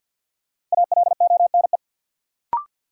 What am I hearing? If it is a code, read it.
Morse code, heard as a keyed tone, about 47 wpm, 700 Hz. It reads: ALONE
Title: Sentences from Top 1000 Words - New Word Drill - 50wpm